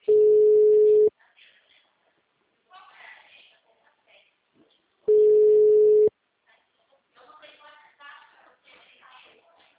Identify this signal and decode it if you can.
Ringback tone (Europe, 425 Hz), with speech.
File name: out- -2011- - - .8746